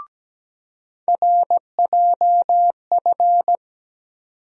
Morse code, heard as a keyed tone, about 17 wpm, 700 Hz. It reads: RJF
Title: USA Call Sign Postfixes - 3 Letters 17wpm